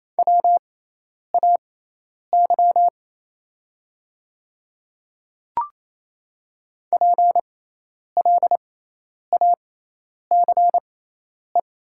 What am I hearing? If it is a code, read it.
Morse code, heard as a keyed tone, about 28 wpm, spaced as for 10 wpm, 700 Hz. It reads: WAY PLACE
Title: 6x Character Spacing - Top 500 Words - 28wpm